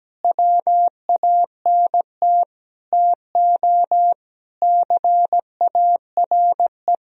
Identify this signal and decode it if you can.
Morse code, heard as a keyed tone, about 17 wpm, 700 Hz. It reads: WANT TO CARE
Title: Sentences from Top 1000 Words - Review Part 2 - 17wpm